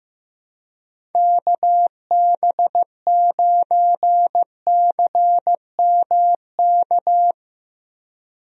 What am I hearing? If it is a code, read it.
Morse code, heard as a keyed tone, about 15 wpm, 700 Hz. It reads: KB9CMK
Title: USA Call Signs - 6 Characters - Repeated 2x - 15wpm